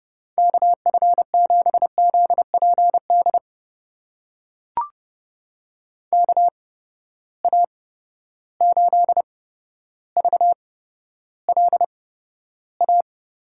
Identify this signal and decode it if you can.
Morse code, heard as a keyed tone, about 30 wpm, spaced as for 13 wpm, 700 Hz. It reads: KF7ZPD KA8VLA
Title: USA Call Signs - 6 Characters - 8x Character Spacing - 30wpm